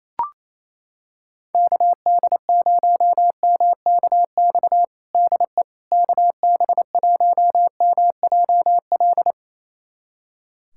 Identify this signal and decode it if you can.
Morse code, heard as a keyed tone, about 28 wpm, 700 Hz. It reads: KD0MKX DE KB1MJL
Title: USA Callsign DE Callsign - 6 Characters - 28wpm